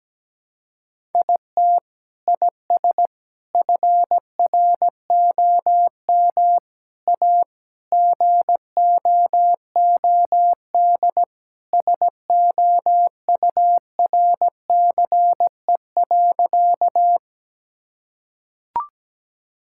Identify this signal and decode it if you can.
Morse code, heard as a keyed tone, about 17 wpm, 700 Hz. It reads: IT IS FROM A GOOD SOURCE.
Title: Sentences from Top 1000 Words - Review Part 5 - 17wpm